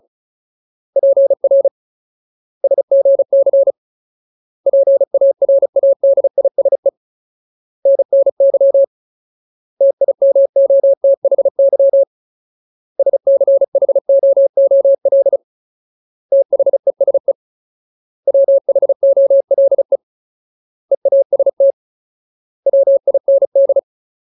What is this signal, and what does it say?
Morse code, heard as a keyed tone, about 34 wpm, 550 Hz. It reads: PR SGC PARADISE NNY TIMOTHY SCHOOL THESE WHOLE EAST WIND
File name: Ultimate Word List / 35WPM Ultimate